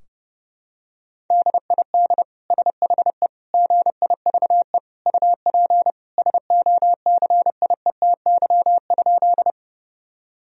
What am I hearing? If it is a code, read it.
Morse code, heard as a keyed tone, about 30 wpm, 700 Hz. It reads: DID SHE GIVE UP SOCIETY?
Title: Sentences from Top 1000 Words - Review Part 4 - 30wpm